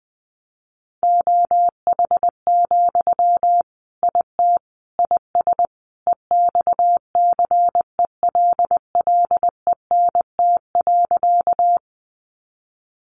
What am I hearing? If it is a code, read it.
Morse code, heard as a keyed tone, about 20 wpm, 700 Hz. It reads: OH, IT IS EXCELLENT.
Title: Sentences from Top 1000 Words 20wpm